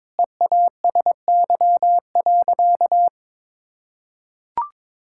Morse code, heard as a keyed tone, about 22 wpm, 700 Hz. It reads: EASY.
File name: practice / 22wpm / Sentences with 4-Word Phrases - 22wpm